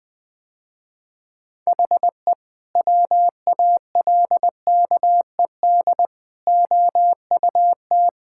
Morse code, heard as a keyed tone, about 19 wpm, 700 Hz. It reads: HE WALKED OUT
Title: Sentences with 2-Word Phrases - Spoken Beginning - 20wpm